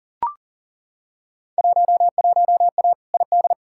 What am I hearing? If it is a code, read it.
Morse code, heard as a keyed tone, about 40 wpm, 700 Hz. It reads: 11A ID
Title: Field Day Exchanges Encore - 40wpm